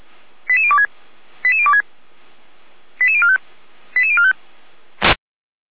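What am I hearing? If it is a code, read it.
Five-tone selective call (ZVEI), heard as quick runs of five tones, tones of about 70 ms. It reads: 88026 88026 88045 88045
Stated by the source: ZVEI five-tone selective call received by SDR, fire brigade Orlamünde (feuerwehr-orlamuende).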